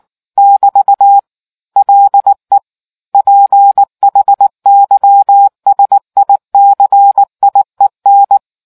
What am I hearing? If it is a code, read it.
Morse code, heard as a keyed tone, about 19 wpm, 800 Hz. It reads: <BT> LE PHYSICIEN